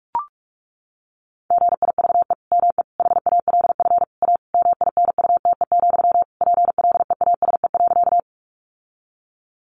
Morse code, heard as a keyed tone, about 44 wpm, 700 Hz. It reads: GIVE ME HALF A MINUTE, PLEASE.
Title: Sentences from Top 1500 Words 45wpm